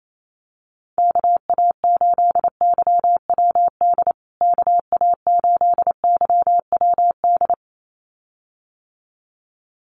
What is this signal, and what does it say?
Morse code, heard as a keyed tone, about 28 wpm, 700 Hz. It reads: KA8YWD KA8YWD
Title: USA Call Signs Repeated 2x 28wpm